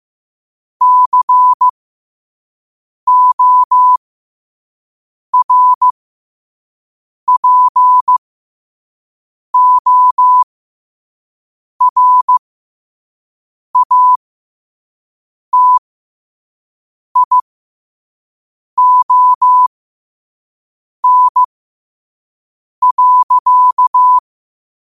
Morse code, heard as a keyed tone, about 15 wpm, spaced as for 5 wpm, 1000 Hz. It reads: CORPORATION.
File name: qotdfiles / 5wpm